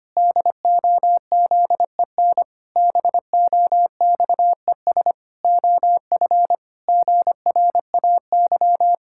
Morse code, heard as a keyed tone, about 25 wpm, 700 Hz. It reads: DOZEN BOXES OF GRAY
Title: Pangram Challenge v2 - 25wpm